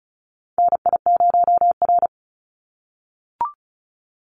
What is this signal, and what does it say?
Morse code, heard as a keyed tone, about 34 wpm, 700 Hz. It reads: NI0R
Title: Call Signs - 35wpm